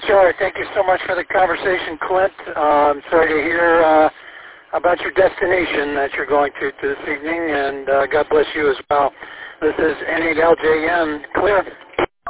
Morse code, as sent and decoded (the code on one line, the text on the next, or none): none